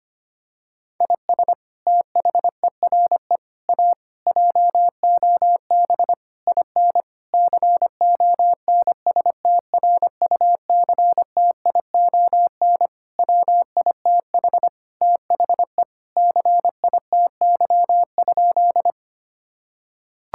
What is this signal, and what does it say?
Morse code, heard as a keyed tone, about 25 wpm, 700 Hz. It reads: IS THERE A JOB IN CONSTRUCTION WITH THE CITY?
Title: Sentences from Top 1600 Words 25wpm